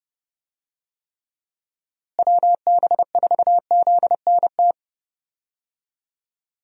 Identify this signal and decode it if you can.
Morse code, heard as a keyed tone, about 30 wpm, 700 Hz. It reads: WB4ZNT
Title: USA Call Signs - 6 Characters - Repeated 2x - 30wpm